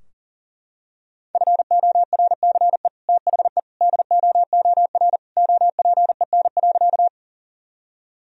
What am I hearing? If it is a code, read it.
Morse code, heard as a keyed tone, about 40 wpm, 700 Hz. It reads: FORCE THE DOOR OPEN.
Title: Sentences from Top 1000 Words - Review Part 3 - 40wpm